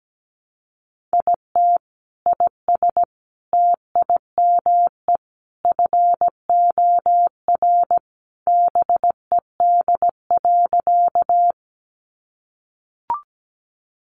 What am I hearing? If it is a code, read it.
Morse code, heard as a keyed tone, about 17 wpm, 700 Hz. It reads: IT IS TIME FOR BED.
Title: Sentences from Top 500 Words 17wpm